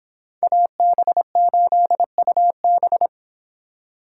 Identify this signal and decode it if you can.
Morse code, heard as a keyed tone, about 29 wpm, 700 Hz. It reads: AB8UB